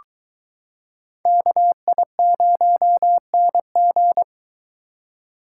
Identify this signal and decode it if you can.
Morse code, heard as a keyed tone, about 23 wpm, 700 Hz. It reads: KI0NG